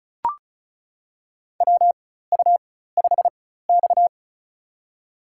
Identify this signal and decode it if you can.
Morse code, heard as a keyed tone, about 36 wpm, spaced as for 16 wpm, 700 Hz. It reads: WU5X
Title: USA Call Signs - 4 Characters - 4x Character Spacing - 35wpm